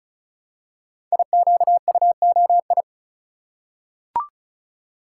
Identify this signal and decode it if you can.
Morse code, heard as a keyed tone, about 36 wpm, 690 Hz. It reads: IQUOI
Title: Code Groups - 5 Letters - 35wpm